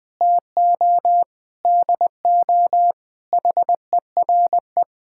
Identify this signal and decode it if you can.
Morse code, heard as a keyed tone, about 21 wpm, 700 Hz. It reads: TO DO HERE
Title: Sentences with 7-Word Phrases - Spoken Beginning - 20wpm